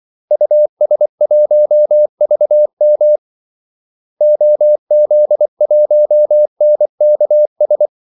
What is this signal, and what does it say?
Morse code, heard as a keyed tone, about 24 wpm, 600 Hz. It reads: US1VM OZ1NKS
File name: Callsigns-24wpm